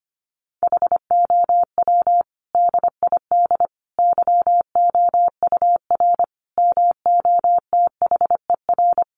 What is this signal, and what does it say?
Morse code, heard as a keyed tone, about 25 wpm, 700 Hz. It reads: HOW DID YOUR MOTHER
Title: Sentences from Top 700 Words 25wpm